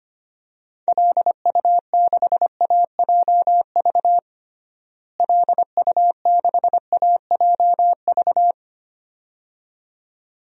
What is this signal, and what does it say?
Morse code, heard as a keyed tone, about 25 wpm, 700 Hz. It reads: LU6AJV LU6AJV
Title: DXCC-100 Call Signs - 6 Characters - Repeated 2x - 25wpm